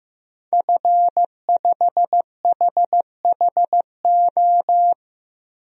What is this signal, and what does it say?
Morse code, heard as a keyed tone, about 15 wpm, 700 Hz. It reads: F5HHO
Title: DXCC-100 Call Signs - 5 Characters - 15wpm